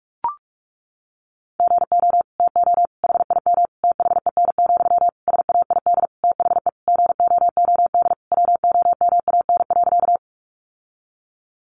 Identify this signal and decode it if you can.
Morse code, heard as a keyed tone, about 44 wpm, 700 Hz. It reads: GO TO HIM THEN, SAID THE GOOD WOMAN.
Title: Sentences from Top 1100 Words 45wpm